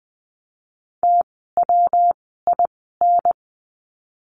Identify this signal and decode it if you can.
Morse code, heard as a keyed tone, about 20 wpm, 700 Hz. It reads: T W I N
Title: 115 Lesson 19f - Letter w - Words 20wpm